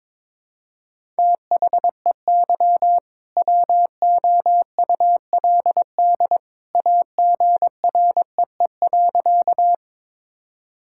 Morse code, heard as a keyed tone, about 22 wpm, 700 Hz. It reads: THEY WOULD AGREE.
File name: practice / 22wpm / Sentences with 2-Word Phrases - 22wpm